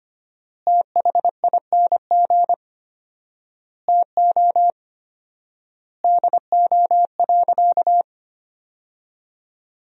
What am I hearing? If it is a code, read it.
Morse code, heard as a keyed tone, about 25 wpm, 700 Hz. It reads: THING TO DO.
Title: 4x Word-Spacing - Sentences from Top 100 Words - 25wpm